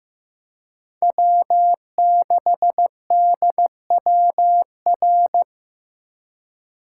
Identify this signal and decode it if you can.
Morse code, heard as a keyed tone, about 15 wpm, 700 Hz. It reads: W6DWR
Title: USA Call Signs - 5 Characters - 3x Character Spacing - 15wpm